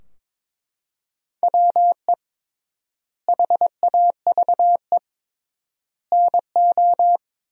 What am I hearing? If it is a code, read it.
Morse code, heard as a keyed tone, about 22 wpm, 700 Hz. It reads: WE HAVE NO
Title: 3x Word-Spacing - Sentences from Top 100 Words - 22wpm